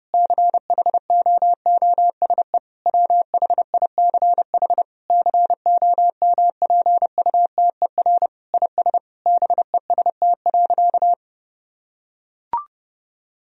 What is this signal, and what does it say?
Morse code, heard as a keyed tone, about 30 wpm, 700 Hz. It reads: CHOOSE WHICH COMPUTER IS BEST.
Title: Sentences from Top 1000 Words - Review Part 3 - 30wpm